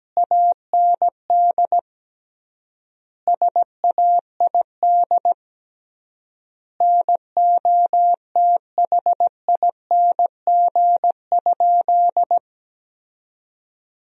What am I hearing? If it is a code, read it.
Morse code, heard as a keyed tone, about 17 wpm, 700 Hz. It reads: AND SAID NOTHING?
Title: 3x Word-Spacing - Sentences from Top 500 Words - 17wpm